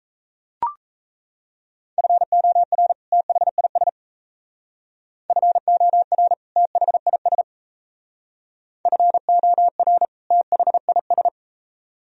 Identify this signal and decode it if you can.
Morse code, heard as a keyed tone, about 35 wpm, 700 Hz. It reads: FOR THIS FOR THIS FOR THIS